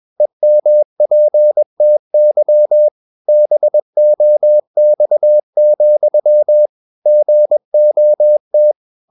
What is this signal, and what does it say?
Morse code, heard as a keyed tone, about 21 wpm, 600 Hz. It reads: EMPTY BOX, GOT
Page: Romanen Never af Ken Follet på engelsk.